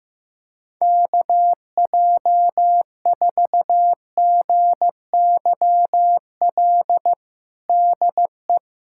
Morse code, heard as a keyed tone, about 15 wpm, 700 Hz. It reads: KJ4GYL DE
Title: USA Callsign DE Callsign - 6 Characters - 15wpm